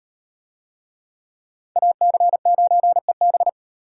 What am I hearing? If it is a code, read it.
Morse code, heard as a keyed tone, about 38 wpm, 700 Hz. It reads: AC9EB